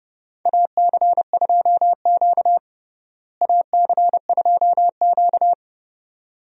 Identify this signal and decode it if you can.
Morse code, heard as a keyed tone, about 30 wpm, 700 Hz. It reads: AC2Q AC2Q
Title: USA Call Signs - 4 Characters - Repeated 2x - 30wpm